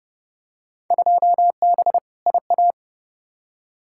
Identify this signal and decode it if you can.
Morse code, heard as a keyed tone, about 30 wpm, 700 Hz. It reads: 2B IA
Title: Field Day Exchanges Encore - 30wpm